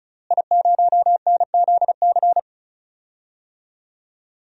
Morse code, heard as a keyed tone, about 36 wpm, 700 Hz. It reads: I0NZC
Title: DXCC-100 Call Signs - 5 Characters - Repeated 3x - 35wpm